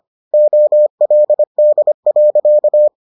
Morse code, heard as a keyed tone, about 25 wpm, 600 Hz. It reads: OLD.